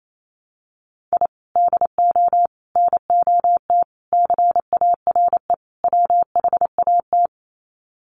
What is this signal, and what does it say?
Morse code, heard as a keyed tone, about 28 wpm, 700 Hz. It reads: I DO NOT CARE WHAT
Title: Sets of 5 Words 28wpm